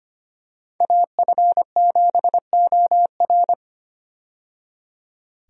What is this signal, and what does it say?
Morse code, heard as a keyed tone, about 25 wpm, 700 Hz. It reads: AF7OR
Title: USA Call Signs - 5 Characters - 25wpm